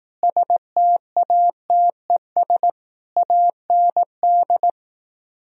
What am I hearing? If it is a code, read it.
Morse code, heard as a keyed tone, about 18 wpm, 700 Hz. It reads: STATES AND
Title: Speed-Racing - Sets of 4 Words 15wpm